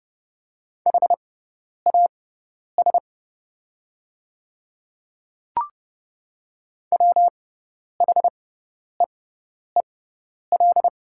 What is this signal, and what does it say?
Morse code, heard as a keyed tone, about 30 wpm, spaced as for 10 wpm, 700 Hz. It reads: HAS WHEEL
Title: 6x Character Spacing - Top 500 Words - 30wpm